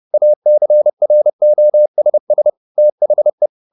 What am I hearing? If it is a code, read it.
Morse code, heard as a keyed tone, about 30 wpm, 600 Hz. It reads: ACROSS THE